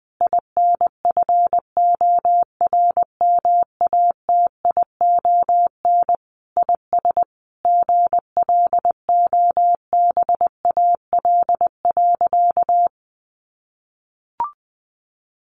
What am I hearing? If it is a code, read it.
Morse code, heard as a keyed tone, about 19 wpm, 700 Hz. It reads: INFORMATION IS GLOBAL.
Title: Sentences from Top 1200 Words 20wpm